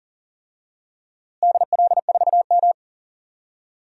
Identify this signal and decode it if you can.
Morse code, heard as a keyed tone, about 39 wpm, 700 Hz. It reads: DL4M
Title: DXCC-100 Call Signs - 4 Characters - Repeated 2x - 40wpm